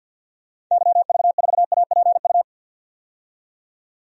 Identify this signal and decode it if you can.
Morse code, heard as a keyed tone, about 51 wpm, 700 Hz. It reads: YV4APU
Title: DXCC-100 Call Signs - 6 Characters - 50wpm